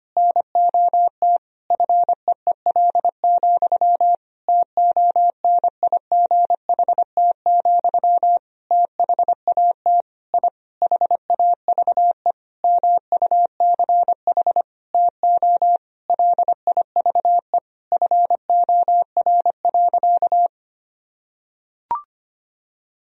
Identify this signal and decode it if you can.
Morse code, heard as a keyed tone, about 25 wpm, 700 Hz. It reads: NOT FEEL, TONIGHT, THAT I HAVE MUCH TO LIVE FOR.
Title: Sentences from Top 1000 Words - Review Part 3 - 25wpm